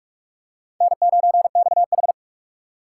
Morse code, heard as a keyed tone, about 41 wpm, 700 Hz. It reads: N9XH